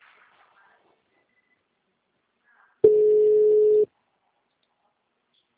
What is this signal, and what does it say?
Ringback tone (Europe, 425 Hz).
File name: out- -2011- - - .57156